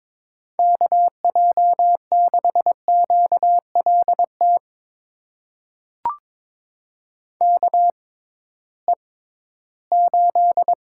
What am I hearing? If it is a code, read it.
Morse code, heard as a keyed tone, about 22 wpm, spaced as for 14 wpm, 700 Hz. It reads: KJ6QLT KE8
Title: USA Call Signs - 6x Character Spacing - 22wpm